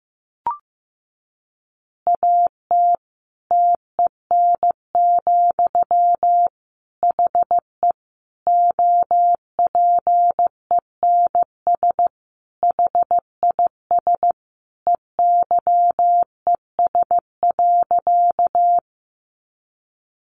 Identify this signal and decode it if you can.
Morse code, heard as a keyed tone, about 14 wpm, 700 Hz. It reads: AT TEN, HE OPENS HIS EYES.
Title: Easy Phrases 15wpm